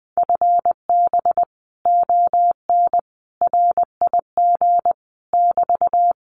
Morse code, heard as a keyed tone, about 20 wpm, 700 Hz. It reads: FB ON RIG <BT>
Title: Common QSO Phrases Encore 20wpm